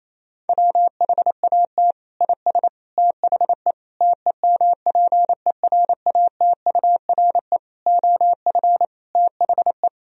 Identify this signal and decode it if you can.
Morse code, heard as a keyed tone, about 28 wpm, 700 Hz. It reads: WHAT IS THE TEMPERATURE OF THE
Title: Sentences from Top 1000 Words - Review Part 4 - 28wpm